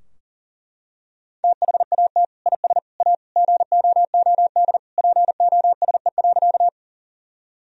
Morse code, heard as a keyed tone, about 40 wpm, 700 Hz. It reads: THAT IS A GOOD POSE.
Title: Sentences from Top 1000 Words - Review Part 2 - 40wpm